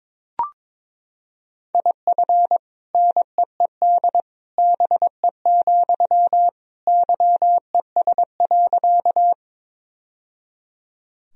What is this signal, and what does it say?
Morse code, heard as a keyed tone, about 22 wpm, 700 Hz. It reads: IF NEED BE, YES.
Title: Sentences from Top 1000 Words - Review Part 1 - 22wpm